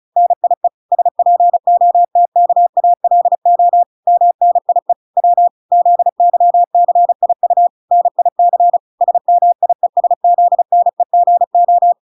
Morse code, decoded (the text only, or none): NIE SPOTKALO MNIE W ZYCIU NIC SMIESZNEGO